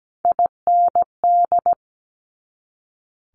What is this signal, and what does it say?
Morse code, heard as a keyed tone, about 17 wpm, 700 Hz. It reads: IND
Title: Multi-letter Phonemes 17wpm